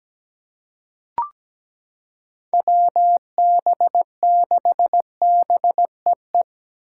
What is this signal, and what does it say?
Morse code, heard as a keyed tone, about 17 wpm, 700 Hz. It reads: WB6BEE
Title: K1USN SST QSO Practice - 17wpm